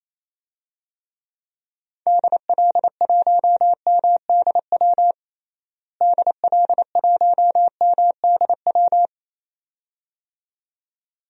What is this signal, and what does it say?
Morse code, heard as a keyed tone, about 27 wpm, 700 Hz. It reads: DL1MDW DL1MDW
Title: DXCC-100 Call Signs - 6 Characters - Repeated 2x - 28wpm